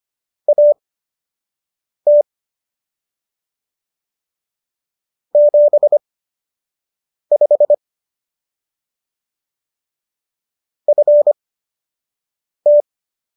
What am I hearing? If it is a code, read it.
Morse code, heard as a keyed tone, about 25 wpm, spaced as for 6 wpm, 590 Hz. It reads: AT 75 FT